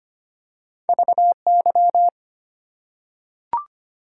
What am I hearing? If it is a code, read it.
Morse code, heard as a keyed tone, about 25 wpm, 690 Hz. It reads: VY
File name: Top 90 QSO Elements - 8x Character Spacing - 25wpm